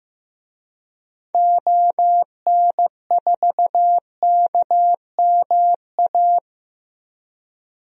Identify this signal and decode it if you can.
Morse code, heard as a keyed tone, about 15 wpm, 700 Hz. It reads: ON4KMA
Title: DXCC-100 Call Signs - 6 Characters - 15wpm